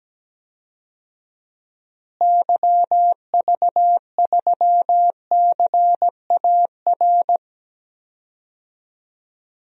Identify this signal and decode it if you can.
Morse code, heard as a keyed tone, about 17 wpm, 700 Hz. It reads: YV3CAR
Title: DXCC-100 Call Signs - 6 Characters - Repeated 2x - 17wpm